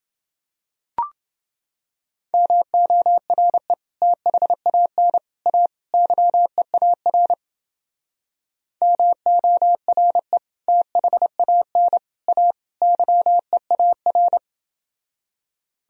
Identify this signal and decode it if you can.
Morse code, heard as a keyed tone, about 29 wpm, 700 Hz. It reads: MORE THAN A YEAR MORE THAN A YEAR